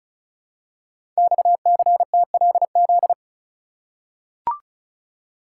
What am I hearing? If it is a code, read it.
Morse code, heard as a keyed tone, about 36 wpm, 700 Hz. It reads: XCTLZ
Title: Code Groups - 5 Letters - 35wpm